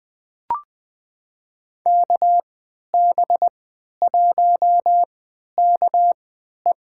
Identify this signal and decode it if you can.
Morse code, heard as a keyed tone, about 21 wpm, spaced as for 11 wpm, 700 Hz. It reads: KB1KE
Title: USA Call Signs - 5 Characters - 3x Character Spacing - 20wpm